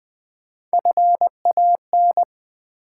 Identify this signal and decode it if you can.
Morse code, heard as a keyed tone, about 20 wpm, 700 Hz. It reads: FAN
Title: Easy Words with Hints - 20wpm